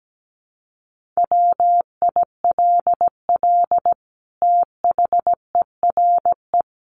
Morse code, heard as a keyed tone, about 17 wpm, 700 Hz. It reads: WILL THERE
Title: Sentences from Top 1400 Words 17wpm